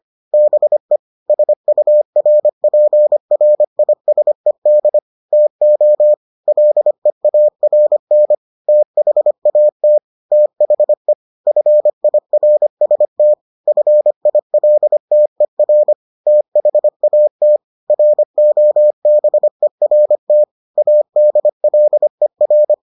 Morse code, heard as a keyed tone, about 25 wpm, 600 Hz. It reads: BE SURPRISED TO LEARN THAT THE FIRST FILTER THAT ROBERT ADLER